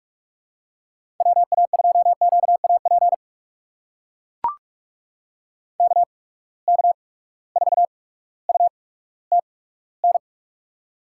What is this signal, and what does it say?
Morse code, heard as a keyed tone, about 44 wpm, spaced as for 19 wpm, 700 Hz. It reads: WA2QAP KK4UTN
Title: USA Call Signs - 6 Characters - 8x Character Spacing - 45wpm